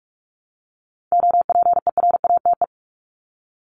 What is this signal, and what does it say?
Morse code, heard as a keyed tone, about 44 wpm, 700 Hz. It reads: OPERATE
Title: ING Suffix - Top 100 Words with Base Words 45wpm